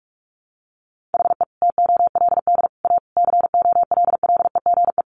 Morse code, heard as a keyed tone, about 44 wpm, 700 Hz. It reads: HE TOLD A COLLEGE
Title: Sentences from Top 1300 Words 45wpm